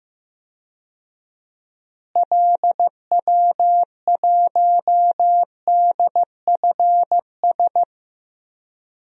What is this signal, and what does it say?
Morse code, heard as a keyed tone, about 15 wpm, 700 Hz. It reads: LW1DFS
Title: DXCC-100 Call Signs - 6 Characters - Repeated 2x - 15wpm